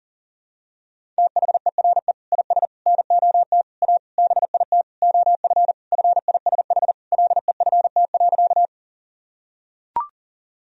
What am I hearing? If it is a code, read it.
Morse code, heard as a keyed tone, about 40 wpm, 700 Hz. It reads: THERE IS NOT A BIT OF FISH LEFT.
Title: Sentences from Top 1000 Words - Review Part 1 - 40wpm